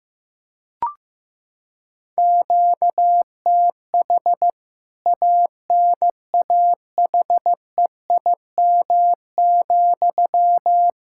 Morse code, heard as a keyed tone, about 15 wpm, 700 Hz. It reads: QTH ANAHEIM,
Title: K1USN SST QSO Practice 15wpm